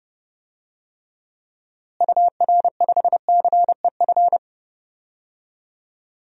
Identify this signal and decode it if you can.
Morse code, heard as a keyed tone, about 32 wpm, 700 Hz. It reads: UR5CEF
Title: DXCC-100 Call Signs - 6 Characters - Repeated 3x - 30wpm